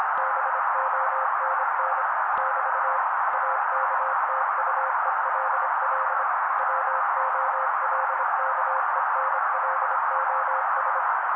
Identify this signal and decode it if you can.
Morse code, heard as a keyed tone, about 26 wpm, 530 Hz. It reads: BONN <BT> AKTUELL WOLKENLOS
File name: daily / 25wpm noise